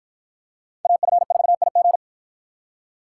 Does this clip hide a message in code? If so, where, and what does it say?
Yes: Morse code from 0.9 to 2.0 s, heard as a keyed tone, about 53 wpm, 700 Hz. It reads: AF4IG